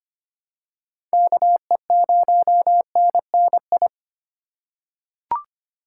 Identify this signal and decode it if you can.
Morse code, heard as a keyed tone, about 25 wpm, 700 Hz. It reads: KE0NNI
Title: USA Call Signs - 6 Characters - 2x Character Spacing - 25wpm